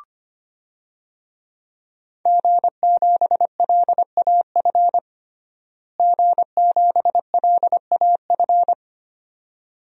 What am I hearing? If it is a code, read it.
Morse code, heard as a keyed tone, about 25 wpm, 700 Hz. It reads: G7LAF G7LAF
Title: DXCC-100 Call Signs - 5 Characters - Repeated 2x - 25wpm